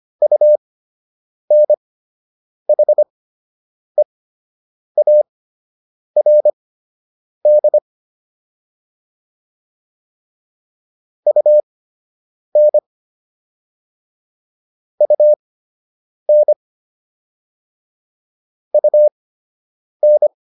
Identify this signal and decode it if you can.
Morse code, heard as a keyed tone, about 25 wpm, spaced as for 8 wpm, 600 Hz. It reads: UNHEARD UN UN UN